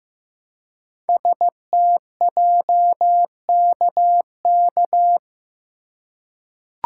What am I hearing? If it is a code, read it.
Morse code, heard as a keyed tone, about 15 wpm, 690 Hz. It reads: STJKK
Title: Code Groups - 5 Letters - 15wpm